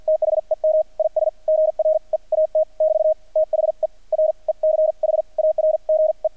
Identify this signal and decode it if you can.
Morse code, heard as a keyed tone, about 48 wpm, spaced as for 39 wpm, 630 Hz. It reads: THEM IS GREAT <BT> THE RECHARGE
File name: archived / CRSnet